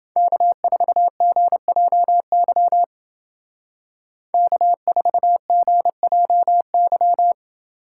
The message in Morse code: K4GJY K4GJY

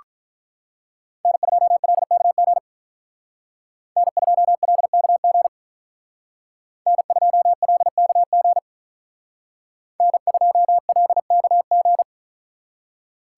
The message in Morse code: N2LKG N2LKG N2LKG N2LKG

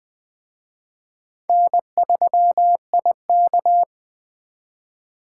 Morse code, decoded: N3IK

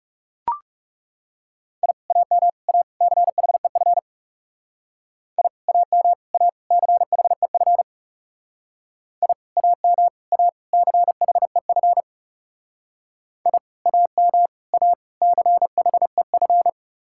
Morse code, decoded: I AM A CHEF I AM A CHEF I AM A CHEF I AM A CHEF

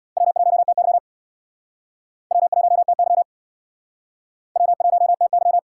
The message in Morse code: W1TY W1TY W1TY

